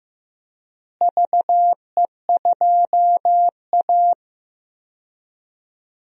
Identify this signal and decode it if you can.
Morse code, heard as a keyed tone, about 15 wpm, 700 Hz. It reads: VE2A
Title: DXCC-100 Call Signs - 4 Characters - Repeated 3x - 15wpm